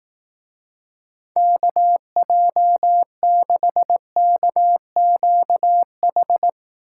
Morse code, heard as a keyed tone, about 18 wpm, 700 Hz. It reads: KJ6KQH
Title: Speed Racing - USA 2x3 Call Signs 15wpm